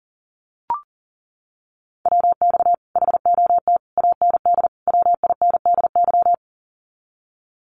Morse code, heard as a keyed tone, about 38 wpm, 700 Hz. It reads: WX HOT AND WINDY
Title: Common QSO Phrases Encore 40wpm